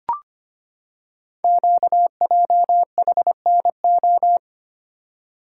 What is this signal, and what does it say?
Morse code, heard as a keyed tone, about 25 wpm, 700 Hz. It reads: QJHNO
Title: Code Groups - 5 Letters - 25wpm